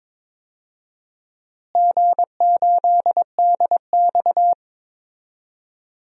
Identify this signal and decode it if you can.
Morse code, heard as a keyed tone, about 22 wpm, 700 Hz. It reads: G8DX
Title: DXCC-100 Call Signs - 4 Characters - Repeated 2x - 22wpm